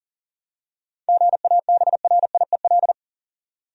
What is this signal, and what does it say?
Morse code, heard as a keyed tone, about 40 wpm, 700 Hz. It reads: GABRIEL
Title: USA Common Names - 40wpm